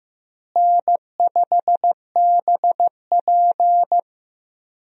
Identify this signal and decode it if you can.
Morse code, heard as a keyed tone, about 16 wpm, 700 Hz. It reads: N5BP
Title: USA Call Signs - 4 Characters - Repeated 3x - 15wpm